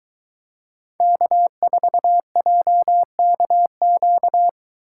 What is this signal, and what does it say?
Morse code, heard as a keyed tone, about 21 wpm, 700 Hz. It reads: K4JKQ